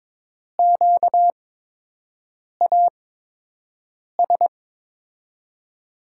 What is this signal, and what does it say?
Morse code, heard as a keyed tone, about 22 wpm, spaced as for 6 wpm, 700 Hz. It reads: QAS
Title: USA Call Signs - 6 Characters - 8x Character Spacing - 22wpm